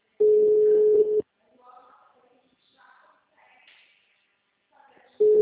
Ringback tone (Europe, 425 Hz).